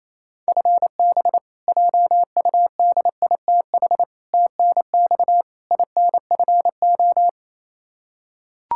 Morse code, heard as a keyed tone, about 27 wpm, 700 Hz. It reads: FB JUDITH TNX INFO
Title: K1USN SST QSO Practice 28wpm